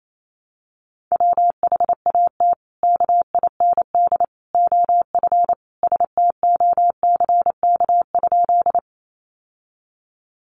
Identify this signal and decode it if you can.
Morse code, heard as a keyed tone, about 28 wpm, 700 Hz. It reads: WHAT KIND OF STOCK?
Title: Sentences from Top 800 Words 28wpm